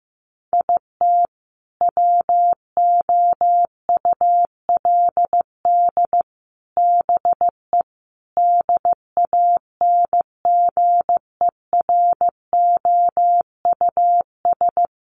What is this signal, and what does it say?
Morse code, heard as a keyed tone, about 14 wpm, 700 Hz. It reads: IT WOULD BE DANGEROUS
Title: Sentences from Top 1200 Words 15wpm